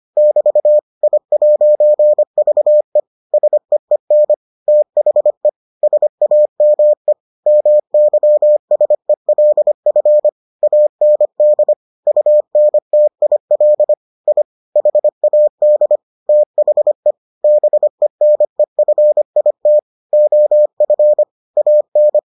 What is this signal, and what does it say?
Morse code, heard as a keyed tone, about 25 wpm, 600 Hz. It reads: <BT> I'VE SEEN THE SAME MYSELF AND UNTIL I HAD THE BENEFIT OF AN